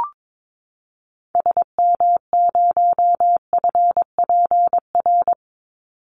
Morse code, heard as a keyed tone, about 22 wpm, 700 Hz. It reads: SM0FPR
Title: Call Signs Repeated 2x - 22wpm